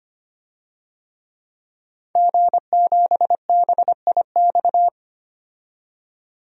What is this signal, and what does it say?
Morse code, heard as a keyed tone, about 25 wpm, 700 Hz. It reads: G7BIX
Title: DXCC-100 Call Signs - 5 Characters - 25wpm